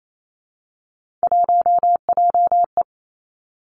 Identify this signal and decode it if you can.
Morse code, heard as a keyed tone, about 28 wpm, 690 Hz. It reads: 1JE
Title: Three Challenging Letter-Number Combinations 28wpm